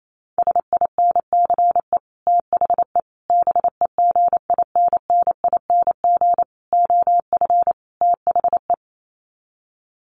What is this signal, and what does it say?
Morse code, heard as a keyed tone, about 28 wpm, 700 Hz. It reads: SINCE THE BEGINNING OF THE
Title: Sets of 5 Words Encore2 28wpm